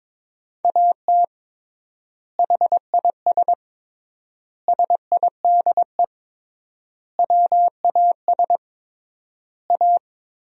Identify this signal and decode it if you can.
Morse code, heard as a keyed tone, about 22 wpm, 700 Hz. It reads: AT HIS SIDE WAS A
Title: 3x Word-Spacing - Sentences from Top 500 Words - 22wpm